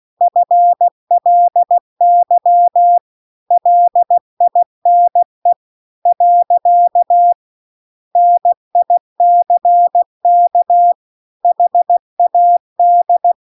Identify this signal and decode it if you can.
Morse code, heard as a keyed tone, about 16 wpm, 700 Hz. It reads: FLY LINE . NICK HAD
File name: CW-The-Big-Twoheaded-River-2-16WPM